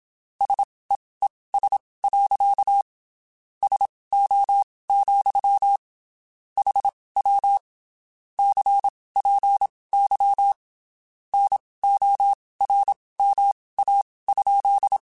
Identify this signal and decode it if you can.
Morse code, heard as a keyed tone, about 27 wpm, spaced as for 19 wpm, 800 Hz. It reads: SEES. SO, HW CPY NORMA?
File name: (spaced) / QSOs Spaced-mp3s / WD0FBT 1 @25 sp